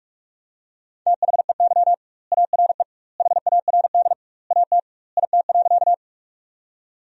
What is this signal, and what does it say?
Morse code, heard as a keyed tone, about 43 wpm, 700 Hz. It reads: THEY ARE HARD AT IT.
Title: 2x Word-Spacing - Sentences from Top 300 Words - 45wpm